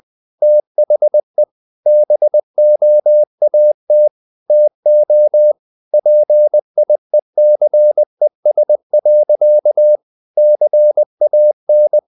Morse code, heard as a keyed tone, about 20 wpm, 600 Hz. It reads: THE BOAT TO PIECES. CAN
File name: the old man and the sea / the old man and the sea part 1 20wpm